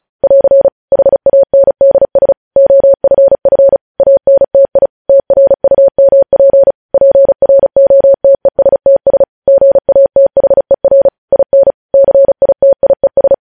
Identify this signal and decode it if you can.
Morse code, heard as a keyed tone, about 36 wpm, 550 Hz. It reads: <AR> HANDS OFF ANTI TRUMP PROTESTS GATHER IN CITIES
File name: podcast / News headlines-1x-35x35